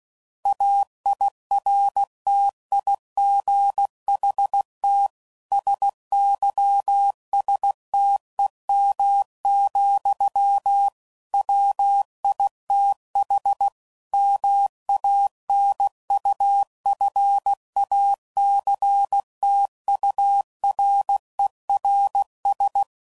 Morse code, heard as a keyed tone, about 16 wpm, 800 Hz. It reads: AIRTIGHT SYSTEM, WITH MANUFACTURERS
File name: Sayings / Quotes / Airtight